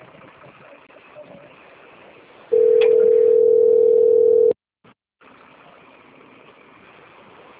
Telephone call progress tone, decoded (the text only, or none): ringback tone